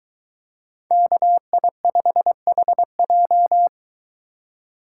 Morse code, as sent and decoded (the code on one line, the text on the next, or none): -.- .. ..... .... .---
KI5HJ